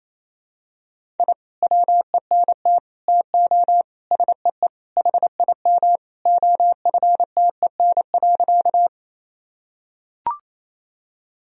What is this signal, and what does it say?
Morse code, heard as a keyed tone, about 27 wpm, 700 Hz. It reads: I WENT TO SEE HIM OFTEN.
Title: Sentences from Top 1000 Words - Review Part 1 - 28wpm